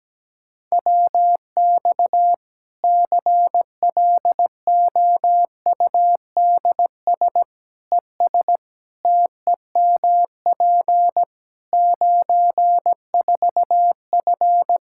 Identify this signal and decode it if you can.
Morse code, heard as a keyed tone, about 17 wpm, 700 Hz. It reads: WX CLOUDS ES TEMP 94F
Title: Practice QSO - Exchange 2 - Info - 17wpm